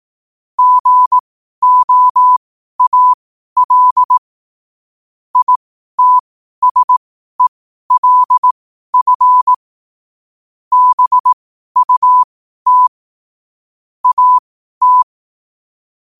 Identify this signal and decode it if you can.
Morse code, heard as a keyed tone, about 18 wpm, spaced as for 12 wpm, 1000 Hz. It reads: GOAL ITSELF BUT AT
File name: qotdfiles / 12wpm